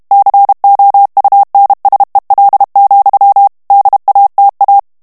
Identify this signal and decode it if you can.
Morse code, heard as a keyed tone, about 32 wpm, 800 Hz. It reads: COUNSEL, DATA
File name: ANGLAIS texte clair 3 vitesse 30 mots minute